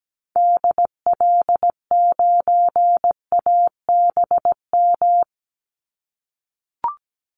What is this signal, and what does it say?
Morse code, heard as a keyed tone, about 17 wpm, 700 Hz. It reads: DL9ABM
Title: Call Signs - 17wpm